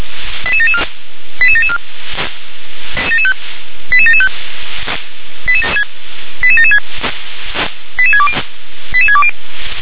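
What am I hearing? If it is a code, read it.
Five-tone selective call (ZVEI), heard as quick runs of five tones, tones of about 70 ms. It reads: 88774 88774 88775 88776 88620 88620
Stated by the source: Five-tone ZVEI selective call received on SDR.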